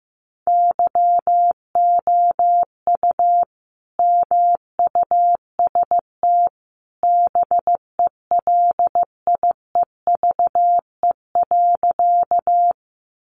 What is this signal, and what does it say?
Morse code, heard as a keyed tone, about 15 wpm, 700 Hz. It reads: YOU MUST BELIEVE.
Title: Sentences from Top 1000 Words 15wpm